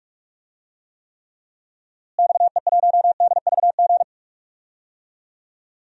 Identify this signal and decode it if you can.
Morse code, heard as a keyed tone, about 45 wpm, 700 Hz. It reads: XE1DVG
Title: DXCC-100 Call Signs - 6 Characters - Repeated 3x - 45wpm